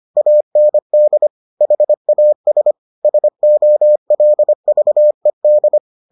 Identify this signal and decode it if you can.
Morse code, heard as a keyed tone, about 25 wpm, 600 Hz. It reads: AND HAS SOLVED